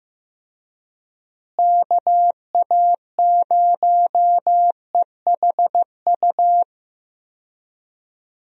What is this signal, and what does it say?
Morse code, heard as a keyed tone, about 15 wpm, 700 Hz. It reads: KA0EHU
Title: USA Call Signs - 6 Characters - Repeated 3x - 15wpm